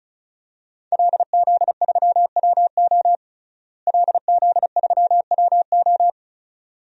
Morse code, heard as a keyed tone, about 34 wpm, 700 Hz. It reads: LZ3WO LZ3WO
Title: DXCC-100 Call Signs - 5 Characters - Repeated 2x - 35wpm